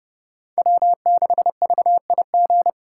Morse code, heard as a keyed tone, about 29 wpm, 700 Hz. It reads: W6VIG
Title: USA Call Signs - 5 Characters - 2x Character Spacing - 30wpm